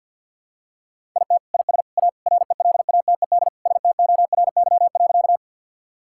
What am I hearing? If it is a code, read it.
Morse code, heard as a keyed tone, about 50 wpm, 690 Hz. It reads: IT IS A RELATED STORY.
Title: Sentences from Top 1000 Words - Review Part 5 - 50wpm